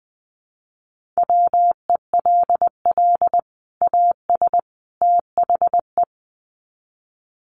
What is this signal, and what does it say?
Morse code, heard as a keyed tone, about 20 wpm, 700 Hz. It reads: WELL AS THE
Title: Sets of 3 Words 20wpm